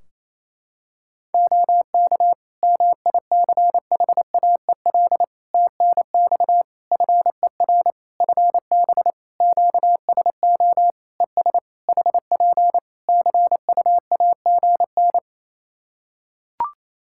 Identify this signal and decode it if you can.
Morse code, heard as a keyed tone, about 28 wpm, 700 Hz. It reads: OK MICHAEL TNX FER FB QSO ES HP CUAGN
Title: K1USN SST QSO Practice 28wpm